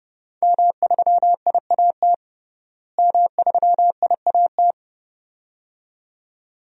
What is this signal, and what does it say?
Morse code, heard as a keyed tone, about 30 wpm, 700 Hz. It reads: M3IAT M3IAT
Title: DXCC-100 Call Signs - 5 Characters - Repeated 2x - 30wpm